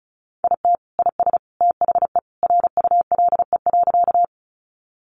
Morse code, heard as a keyed tone, about 34 wpm, 700 Hz. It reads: IT IS THE RULE.
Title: Sentences from Top 400 Words 35wpm